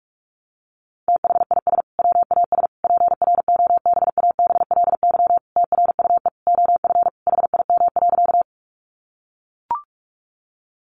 Morse code, heard as a keyed tone, about 44 wpm, 700 Hz. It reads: THIS WAS PROBABLY TRUE OF HIM.
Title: Sentences from Top 700 Words 45wpm